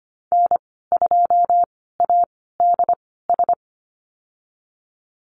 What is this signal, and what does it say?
Morse code, heard as a keyed tone, about 25 wpm, spaced as for 16 wpm, 700 Hz. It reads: N2ADS
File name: Lesson 15g - Number 5 - Call Signs 25wpm